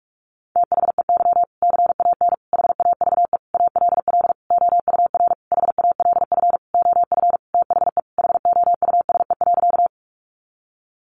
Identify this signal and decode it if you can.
Morse code, heard as a keyed tone, about 44 wpm, 700 Hz. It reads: THEY CAN HAVE ALL OUR HALF OF THE HOUSE.
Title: Sentences from Top 400 Words 45wpm